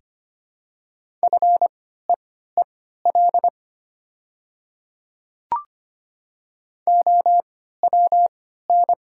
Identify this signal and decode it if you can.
Morse code, heard as a keyed tone, about 25 wpm, spaced as for 14 wpm, 700 Hz. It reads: FEEL OWN
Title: 3x Character Spacing - Top 500 Words - 25wpm